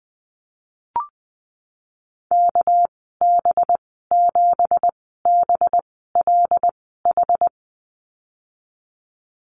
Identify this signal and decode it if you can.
Morse code, heard as a keyed tone, about 20 wpm, spaced as for 15 wpm, 700 Hz. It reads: KB7BLH
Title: 200 Lesson 32g - Letter k - Call Signs 20wpm